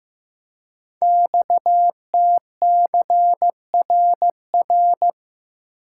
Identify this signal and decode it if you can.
Morse code, heard as a keyed tone, about 15 wpm, 700 Hz. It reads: XTCRR
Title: Code Groups - 5 Letters - 15wpm